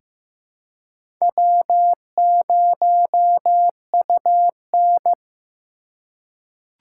Morse code, heard as a keyed tone, about 15 wpm, 700 Hz. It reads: W0UN
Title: USA Call Signs - 4 Characters - 4x Character Spacing - 15wpm